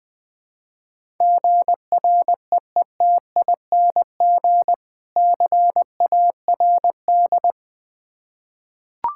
Morse code, heard as a keyed tone, about 20 wpm, 700 Hz. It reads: GREETING CARD
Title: Difficult Words with Hints - 20wpm